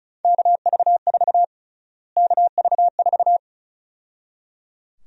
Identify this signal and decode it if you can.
Morse code, heard as a keyed tone, about 34 wpm, 700 Hz. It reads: KV4 KV4
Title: USA Call Sign Prefixes - Repeated 2x - 35wpm